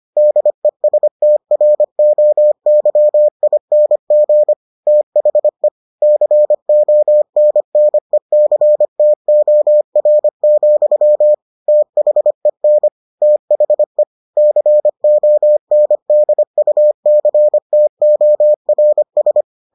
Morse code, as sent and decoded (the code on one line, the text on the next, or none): -.. ... - .-. --- -.-- .. -. --. / - .... / -.-. --- -. -. -.-. - --- .-. --..-- / - .... -. / - .... / -.-. --- -. -.. ..- -.-. - --- .-. ...
DESTROYING THE CONNECTOR, THEN THE CONDUCTORS